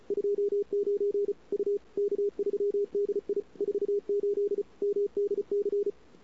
Morse code, heard as a keyed tone, about 35 wpm, 410 Hz. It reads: 29 U K3DI 48 MDC
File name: CW / 2008-ss-cw-k5tr-op-n5rz -22:00